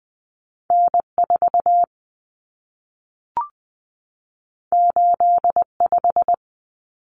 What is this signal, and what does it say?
Morse code, heard as a keyed tone, about 20 wpm, 700 Hz. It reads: N4 85
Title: Common Cut Numbers - 2-Digit 20wpm